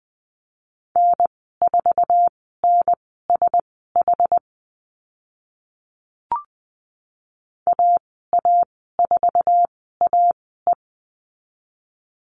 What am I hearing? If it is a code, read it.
Morse code, heard as a keyed tone, about 20 wpm, spaced as for 14 wpm, 700 Hz. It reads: N4NSH AA4AE
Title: 069 Lesson 12g - Letter d - Call Signs 20wpm